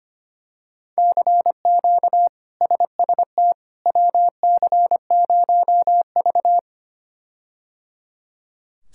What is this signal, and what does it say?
Morse code, heard as a keyed tone, about 25 wpm, 700 Hz. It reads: CQ SST WC0V
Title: K1USN SST QSO Practice - 25wpm